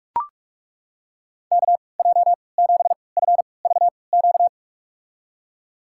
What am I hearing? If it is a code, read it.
Morse code, heard as a keyed tone, about 41 wpm, spaced as for 25 wpm, 700 Hz. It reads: KJ7FVQ
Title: USA Call Signs - 3x Character Spacing - 45wpm